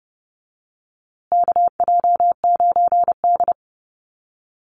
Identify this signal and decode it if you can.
Morse code, heard as a keyed tone, about 30 wpm, 700 Hz. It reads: KJ9D
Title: USA Call Signs Repeated 3x 30wpm